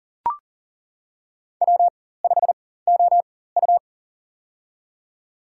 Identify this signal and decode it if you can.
Morse code, heard as a keyed tone, about 40 wpm, spaced as for 19 wpm, 700 Hz. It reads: W5OU
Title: USA Call Signs - 4x Character Spacing - 40wpm